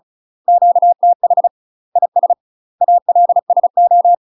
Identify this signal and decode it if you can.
Morse code, heard as a keyed tone, about 34 wpm, 700 Hz. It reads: QTH IS ALSO
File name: UK MASTER QSOs TRAINING / 35WPM / UK MASTER QSOs 0003